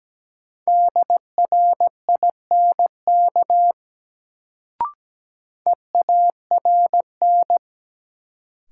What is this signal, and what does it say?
Morse code, heard as a keyed tone, about 17 wpm, 700 Hz. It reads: DRINK EARN
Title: Sentences from Top 1500 Words - New Word Drill - 17wpm